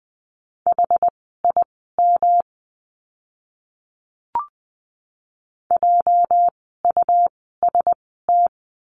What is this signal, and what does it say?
Morse code, heard as a keyed tone, about 20 wpm, spaced as for 15 wpm, 700 Hz. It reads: HIM JUST
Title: 237 Lesson 38f - Letter q - Words 20wpm